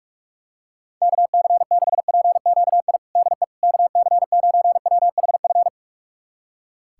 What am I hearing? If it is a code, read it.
Morse code, heard as a keyed tone, about 46 wpm, 700 Hz. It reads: KC6PQI DE KC9WHF
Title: USA Callsign DE Callsign - 6 Characters - 45wpm